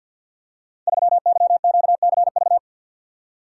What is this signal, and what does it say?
Morse code, heard as a keyed tone, about 52 wpm, 700 Hz. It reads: 3YQCV
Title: Code Groups - 5 Letter:Numbers - 50wpm